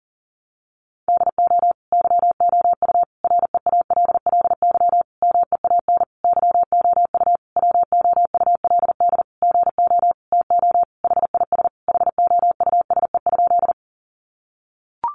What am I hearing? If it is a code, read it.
Morse code, heard as a keyed tone, about 40 wpm, 700 Hz. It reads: DO YOU REALLY MEAN YOU WOULD GO TO HIS HOUSE?
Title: Sentences from Top 600 Words 40wpm